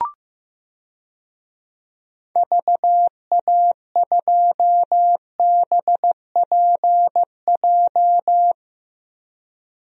Morse code, heard as a keyed tone, about 15 wpm, 700 Hz. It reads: VA2BPJ
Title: DXCC-100 Call Signs - 6 Characters - Repeated 3x - 15wpm